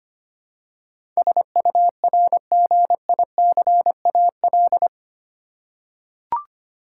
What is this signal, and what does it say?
Morse code, heard as a keyed tone, about 25 wpm, 700 Hz. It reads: SURGICAL
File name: 8 Letter Words - First 3 Letters Given - 25wpm